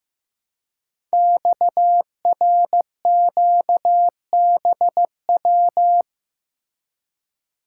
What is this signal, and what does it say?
Morse code, heard as a keyed tone, about 15 wpm, 700 Hz. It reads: XRQBW
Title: Code Groups - 5 Letters - 15wpm